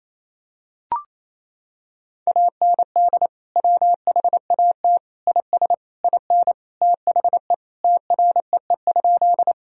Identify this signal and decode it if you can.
Morse code, heard as a keyed tone, about 28 wpm, 700 Hz. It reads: AND WHAT IS IN THE TREE?
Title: Sentences from Top 1000 Words - Review Part 1 - 28wpm